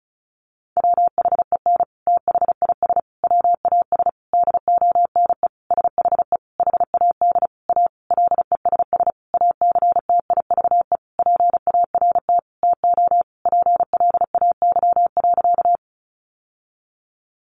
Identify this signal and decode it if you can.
Morse code, heard as a keyed tone, about 34 wpm, 700 Hz. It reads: WHEN THIS WAS DONE SHE HAD A LESS ACTIVE PART TO PLAY.
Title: Sentences from Top 900 Words 35wpm